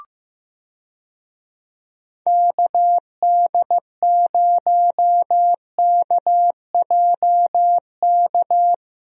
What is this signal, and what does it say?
Morse code, heard as a keyed tone, about 15 wpm, 700 Hz. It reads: KD0KJK
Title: USA Call Signs - 6 Characters - Repeated 3x - 15wpm